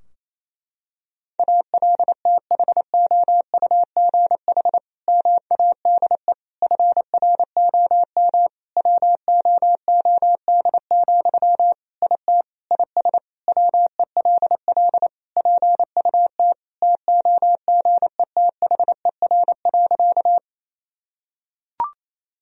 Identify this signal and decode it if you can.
Morse code, heard as a keyed tone, about 28 wpm, 700 Hz. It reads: ALTHOUGH MADE FROM WOOD, IT IS WELL PUT TOGETHER.
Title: Sentences from Top 1000 Words - Review Part 3 - 28wpm